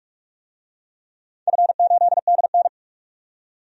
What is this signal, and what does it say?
Morse code, heard as a keyed tone, about 42 wpm, 700 Hz. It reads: F8DN
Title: DXCC-100 Call Signs - 4 Characters - 45wpm